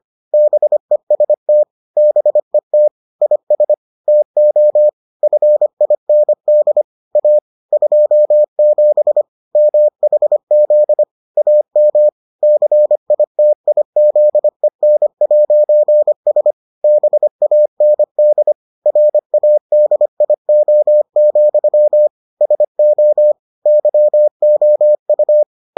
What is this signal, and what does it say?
Morse code, heard as a keyed tone, about 25 wpm, 600 Hz. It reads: BEST BET IS TO FIND A 27 MHZ AM CITIZEN'S BAND RADIO, SO YOU